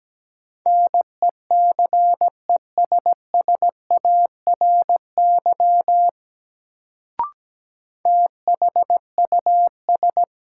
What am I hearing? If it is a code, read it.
Morse code, heard as a keyed tone, about 17 wpm, 700 Hz. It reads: NECESSARY THUS
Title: Sentences from Top 800 Words - New Word Drill - 17wpm